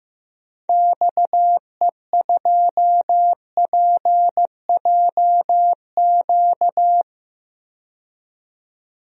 Morse code, heard as a keyed tone, about 15 wpm, 700 Hz. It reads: XE2PJQ
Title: DXCC-100 Call Signs - 6 Characters - 15wpm